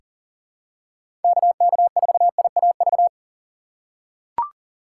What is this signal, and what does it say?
Morse code, heard as a keyed tone, about 40 wpm, 700 Hz. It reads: KK4IAV
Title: USA Call Signs - 6 Characters - 6x Character Spacing - 40wpm